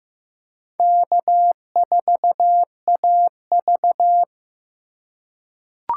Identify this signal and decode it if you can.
Morse code, heard as a keyed tone, about 15 wpm, 700 Hz. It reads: K4AV
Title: USA Call Signs - 4 Characters - 2x Character Spacing - 15wpm